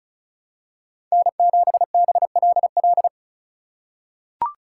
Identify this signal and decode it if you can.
Morse code, heard as a keyed tone, about 36 wpm, 700 Hz. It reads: N7BLL
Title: USA Call Signs - 5 Characters - 2x Character Spacing - 35wpm